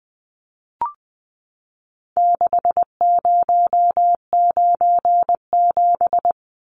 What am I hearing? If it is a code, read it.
Morse code, heard as a keyed tone, about 20 wpm, 700 Hz. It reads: 6097
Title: Numbers - 4-Digit 20wpm